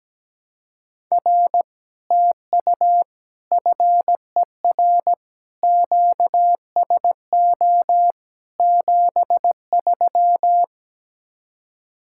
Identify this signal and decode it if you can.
Morse code, heard as a keyed tone, about 17 wpm, 700 Hz. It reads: R TU FER QSO 73
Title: Speed-Racing Common QSO Phrases - 17wpm